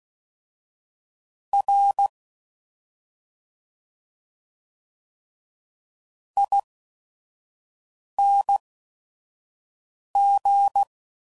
Morse code, heard as a keyed tone, about 16 wpm, spaced as for 5 wpm, 800 Hz. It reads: R ING
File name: Suffixes / Special ING's / Sngl ING's